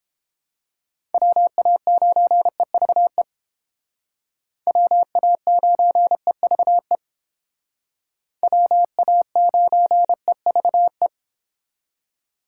Morse code, heard as a keyed tone, about 29 wpm, 700 Hz. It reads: WA9EVE WA9EVE WA9EVE